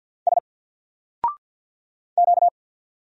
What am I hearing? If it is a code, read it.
Morse code, heard as a keyed tone, about 50 wpm, 700 Hz. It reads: S Q
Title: Single Character (Letter:Number:Punctuation:Prosign) 50wpm